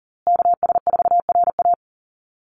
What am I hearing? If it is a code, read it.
Morse code, heard as a keyed tone, about 40 wpm, 700 Hz. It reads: KS4RA
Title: USA Call Signs Repeated 2x 40wpm